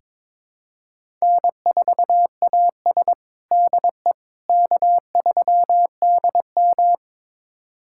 Morse code, heard as a keyed tone, about 22 wpm, 700 Hz. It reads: N4AS DE K3DM
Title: USA Callsign DE Callsign - 4 Characters - 22wpm